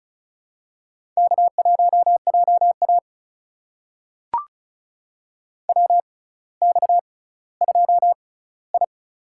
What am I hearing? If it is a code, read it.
Morse code, heard as a keyed tone, about 36 wpm, spaced as for 18 wpm, 700 Hz. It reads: K1JA WX2I
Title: USA Call Signs - 4 Characters - 6x Character Spacing - 35wpm